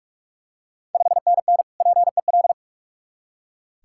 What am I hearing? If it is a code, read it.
Morse code, heard as a keyed tone, about 44 wpm, 690 Hz. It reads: 5NN PEL
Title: Ontario QSO Party Exchange 45wpm